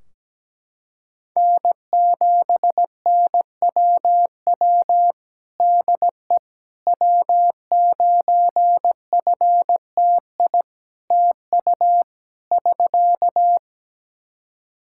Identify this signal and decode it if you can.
Morse code, heard as a keyed tone, about 17 wpm, 700 Hz. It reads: N7NWW DE W9FTI TU <SK>
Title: Practice QSO - Exchange 4 - Ending - 17wpm